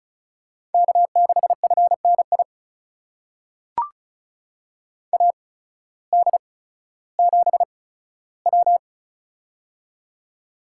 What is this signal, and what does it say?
Morse code, heard as a keyed tone, about 36 wpm, spaced as for 16 wpm, 700 Hz. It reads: K6FNI AD7W
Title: USA Call Signs - 8x Character Spacing - 35wpm